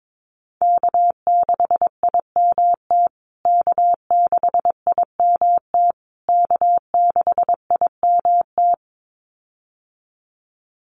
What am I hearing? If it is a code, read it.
Morse code, heard as a keyed tone, about 22 wpm, 700 Hz. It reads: K6IMT K6IMT K6IMT
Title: USA Call Signs Repeated 3x 22wpm